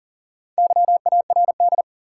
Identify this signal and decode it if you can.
Morse code, heard as a keyed tone, about 40 wpm, 700 Hz. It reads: YARD
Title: Four Letter Words - 40wpm